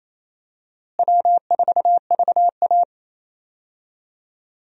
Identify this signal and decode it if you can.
Morse code, heard as a keyed tone, about 28 wpm, 700 Hz. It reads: W4VA
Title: USA Call Signs - 4 Characters - Repeated 3x - 28wpm